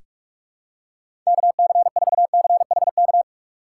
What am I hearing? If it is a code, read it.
Morse code, heard as a keyed tone, about 47 wpm, 700 Hz. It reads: KC4CHK
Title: USA Call Signs - 6 Characters - 6x Character Spacing - 45wpm